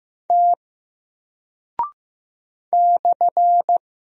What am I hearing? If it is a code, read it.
Morse code, heard as a keyed tone, about 15 wpm, 700 Hz. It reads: T /
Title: Single Character (Letter:Number:Punctuation:Prosign) Alt 15wpm